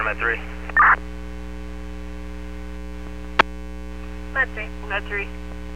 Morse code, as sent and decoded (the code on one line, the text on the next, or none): none